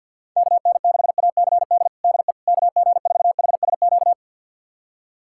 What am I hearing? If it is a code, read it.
Morse code, heard as a keyed tone, about 51 wpm, 700 Hz. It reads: KN6ACD DE KG4HSQ
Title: USA Callsign DE Callsign 50wpm